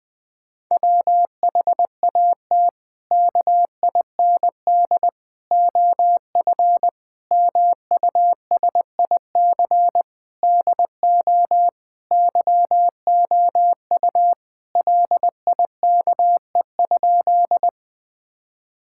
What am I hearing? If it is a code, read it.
Morse code, heard as a keyed tone, about 20 wpm, 700 Hz. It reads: WHAT KIND OF MUSIC DO YOU LIKE?
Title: Sentences with 3-Word Phrases - Spoken Beginning - 20wpm